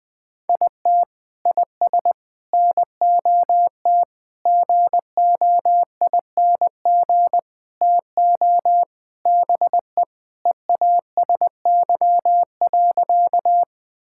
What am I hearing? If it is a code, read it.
Morse code, heard as a keyed tone, about 20 wpm, 700 Hz. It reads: IT IS NOT GOING TO BE EASY.
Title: Sentences with 5-Word Phrases - 20wpm